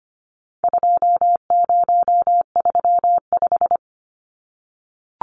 Morse code, heard as a keyed tone, about 25 wpm, 700 Hz. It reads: 2035
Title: Numbers - 4-Digit 25wpm